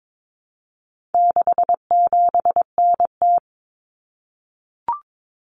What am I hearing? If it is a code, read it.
Morse code, heard as a keyed tone, about 22 wpm, 700 Hz. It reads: 67NT
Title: Common Cut Numbers - 4-Digit 22wpm